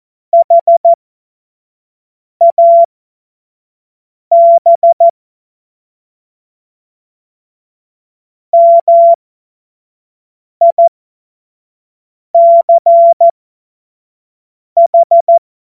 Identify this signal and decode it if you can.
Morse code, heard as a keyed tone, about 14 wpm, spaced as for 5 wpm, 680 Hz. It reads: HAB MICH
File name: Morsekurs / Witze / Witz 06. April 2021, 25 F70